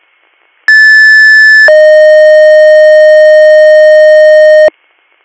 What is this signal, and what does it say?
Two-tone page, heard as a one-second tone, then a three-second tone, about 1750 then 620 Hz. It is heard from 0.7 to 4.7 s.